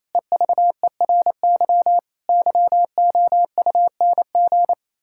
Morse code, heard as a keyed tone, about 28 wpm, 700 Hz. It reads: EVERY YOUNG
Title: Pangram Challenge v2 - 28wpm